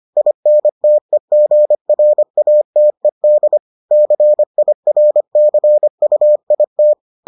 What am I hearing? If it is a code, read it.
Morse code, heard as a keyed tone, about 25 wpm, 600 Hz. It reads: INTEGRATED CIRCUIT